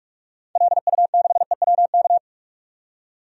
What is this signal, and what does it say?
Morse code, heard as a keyed tone, about 46 wpm, 700 Hz. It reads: LU6EWK
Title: DXCC-100 Call Signs - 6 Characters - 45wpm